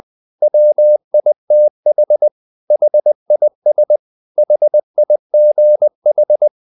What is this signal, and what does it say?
Morse code, heard as a keyed tone, about 20 wpm, 600 Hz. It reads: WITH HIS HIGH
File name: the old man and the sea part 2 20wpm